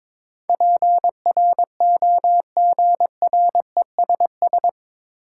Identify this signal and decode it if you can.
Morse code, heard as a keyed tone, about 22 wpm, 700 Hz. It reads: PROGRESS
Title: Sentences from Top 1400 Words - New Word Drill Alt - 22wpm